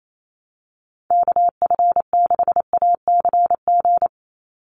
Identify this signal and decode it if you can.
Morse code, heard as a keyed tone, about 28 wpm, 700 Hz. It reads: KF6ACG
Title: USA Call Signs 28wpm